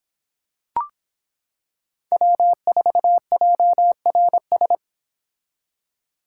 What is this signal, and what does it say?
Morse code, heard as a keyed tone, about 25 wpm, 700 Hz. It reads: W4JRS